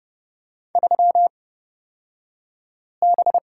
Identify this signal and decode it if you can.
Morse code, heard as a keyed tone, about 30 wpm, spaced as for 5 wpm, 700 Hz. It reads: 3B
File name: Lesson 37d - Letter x - Two Characters 30wpm